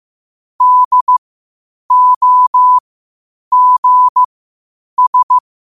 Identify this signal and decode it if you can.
Morse code, heard as a keyed tone, about 15 wpm, spaced as for 8 wpm, 1000 Hz. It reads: DOGS
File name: qotdfiles / 8wpm